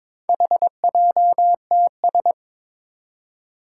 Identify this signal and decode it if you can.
Morse code, heard as a keyed tone, about 23 wpm, 700 Hz. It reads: HJTS
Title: Four Challenging Letter-Number Combinations 22wpm